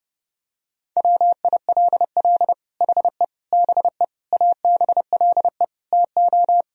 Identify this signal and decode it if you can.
Morse code, heard as a keyed tone, about 32 wpm, 700 Hz. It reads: WILL HE BE ABLE TO
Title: Sentences from Top 1000 Words - Review Part 3 - 30wpm